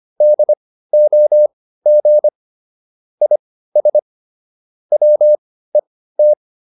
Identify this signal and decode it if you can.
Morse code, heard as a keyed tone, about 25 wpm, spaced as for 15 wpm, 600 Hz. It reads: DOG IS WET